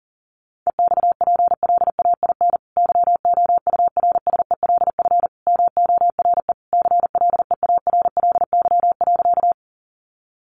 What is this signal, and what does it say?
Morse code, heard as a keyed tone, about 40 wpm, 700 Hz. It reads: EXPLAIN YOURSELF MORE CLEARLY.
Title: Sentences from Top 1300 Words 40wpm